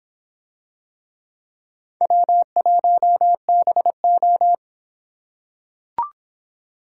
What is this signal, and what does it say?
Morse code, heard as a keyed tone, about 24 wpm, 700 Hz. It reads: W1BO